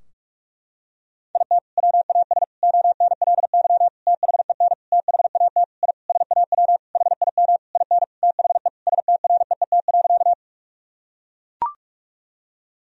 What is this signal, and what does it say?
Morse code, heard as a keyed tone, about 43 wpm, 700 Hz. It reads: IT WAS ONLY THEN THAT I SAW HIM IN THE STREET.
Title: Sentences from Top 1000 Words - Review Part 3 - 45wpm